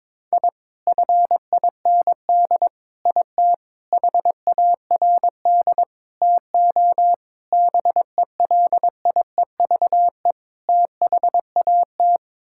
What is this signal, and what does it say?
Morse code, heard as a keyed tone, about 22 wpm, 700 Hz. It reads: I FIND IT HARD TO BELIEVE THAT
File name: practice / 22wpm / Sets of 7 Words - 22wpm